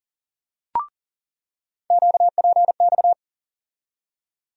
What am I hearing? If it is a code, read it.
Morse code, heard as a keyed tone, about 40 wpm, 700 Hz. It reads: QPX
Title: USA Call Sign Postfixes - 3 Letters 40wpm